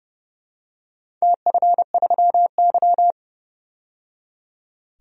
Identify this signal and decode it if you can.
Morse code, heard as a keyed tone, about 30 wpm, 700 Hz. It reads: TF3Y
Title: DXCC-100 Call Signs - 4 Characters - 30wpm